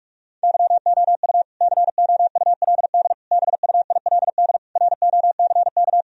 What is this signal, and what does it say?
Morse code, heard as a keyed tone, about 46 wpm, 700 Hz. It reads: YOU COULD BUILD ROCK
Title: Sentences from Top 1000 Words - Review Part 2 - 45wpm